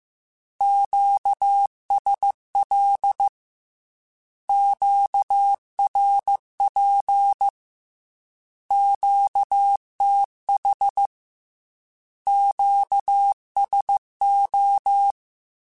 Morse code, heard as a keyed tone, about 15 wpm, 800 Hz. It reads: QSL QRP QTH QSO
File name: Q Calls / Q Calls-mp3s / Q Calls Once @13